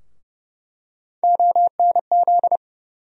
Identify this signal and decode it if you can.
Morse code, heard as a keyed tone, about 32 wpm, 700 Hz. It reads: ONZ
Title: USA Call Sign Postfixes - 3 Letters 30wpm